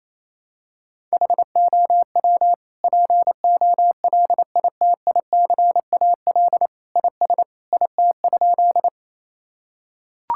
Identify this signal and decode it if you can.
Morse code, heard as a keyed tone, about 28 wpm, 700 Hz. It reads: HOW POLITICAL IS IT?
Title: Sentences from Top 1000 Words - Review Part 5 - 28wpm